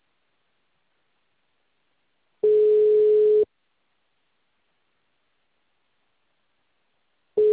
Ringback tone (Europe, 425 Hz).
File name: out- -2039- - - .27532